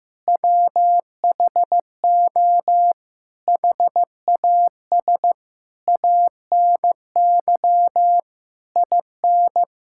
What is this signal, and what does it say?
Morse code, heard as a keyed tone, about 15 wpm, 700 Hz. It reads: WHO HAS ANY IN
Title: Sentences from Top 1600 Words 15wpm